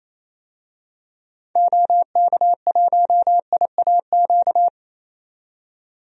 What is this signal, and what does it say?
Morse code, heard as a keyed tone, about 27 wpm, 700 Hz. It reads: OK1IAQ
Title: DXCC-100 Call Signs - 6 Characters - 28wpm